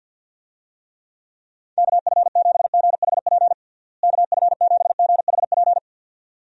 Morse code, heard as a keyed tone, about 46 wpm, 700 Hz. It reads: KF7GHP KF7GHP
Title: USA Call Signs - 6 Characters - Repeated 2x - 50wpm